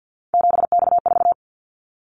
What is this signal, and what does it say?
Morse code, heard as a keyed tone, about 50 wpm, 700 Hz. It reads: 7X4
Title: Callsign Prefixes for Top 100 DXCC - 50wpm